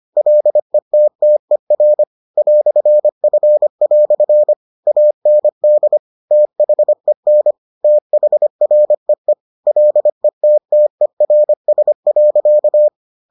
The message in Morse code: LETTER "F" AND THEN THREE LETTERS.

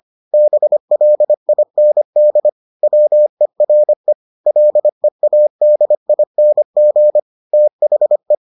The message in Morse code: BLIND WERE LEADING THE

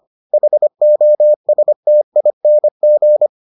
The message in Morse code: HOSTING

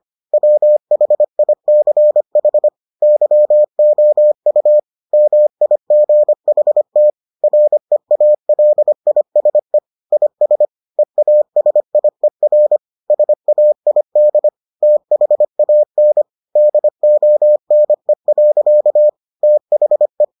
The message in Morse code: WHICH YOU MIGHT REALISE IS EASIER SAID THAN DONE. THE

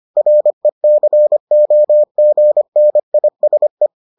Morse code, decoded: RECOGNISE

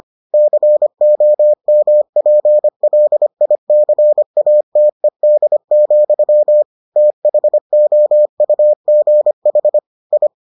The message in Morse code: COMPLICATED, THOUGH I

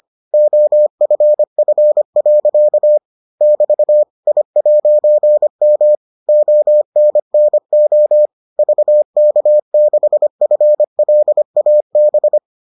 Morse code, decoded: OFF. <BT> I'M ONNO VK6FLAB

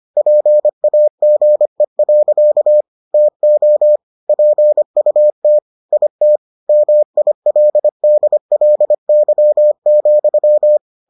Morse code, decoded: PAGE. TO PUT IT MILDLY,

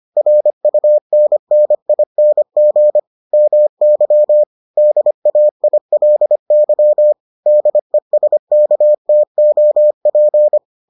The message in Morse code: RUNNING MY DAILY DESKTOP

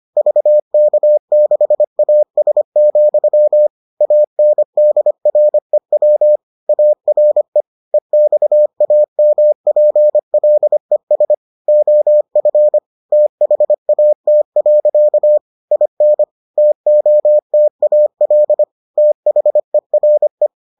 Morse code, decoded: VK6AS, ANDREW ARE EXAMPLES OF THAT. IN TOTAL THERE

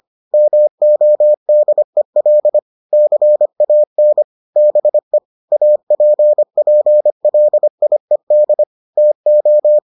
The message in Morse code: MODEL CAN BE APPLIED TO

